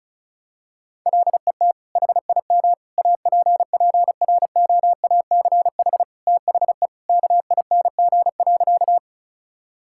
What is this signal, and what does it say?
Morse code, heard as a keyed tone, about 36 wpm, 700 Hz. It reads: LET HIM APPROACH THE KING.
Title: Sentences from Top 1000 Words - Review Part 6 - 35wpm